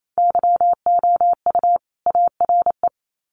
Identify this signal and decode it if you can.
Morse code, heard as a keyed tone, about 28 wpm, 700 Hz. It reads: YOU ARE
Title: Sets of 2 Words 28wpm